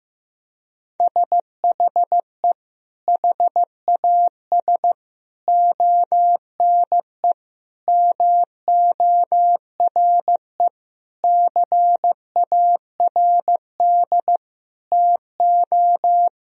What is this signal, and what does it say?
Morse code, heard as a keyed tone, about 15 wpm, 700 Hz. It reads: SHE HAS ONE MORE CARD TO
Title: Sentences from Top 1000 Words - Review Part 4 - 15wpm